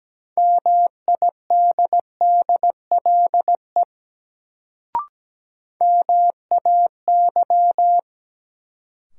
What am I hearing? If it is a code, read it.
Morse code, heard as a keyed tone, about 17 wpm, 700 Hz. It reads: MIDDLE MAY
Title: Sets of 2 Words - New Word Drill - 17wpm